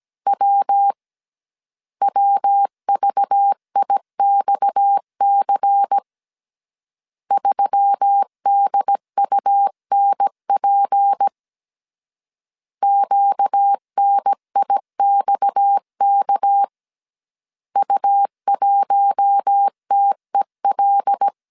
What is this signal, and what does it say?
Morse code, heard as a keyed tone, about 17 wpm, 800 Hz. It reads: W WVIXC 3DUNP QNIXK U1TEL
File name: CW / DET / 1 6 1500C S A 16wpm 2635